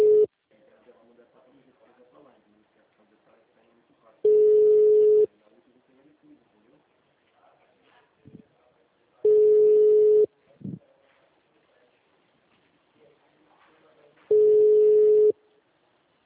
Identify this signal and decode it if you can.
Ringback tone (Europe, 425 Hz), with speech.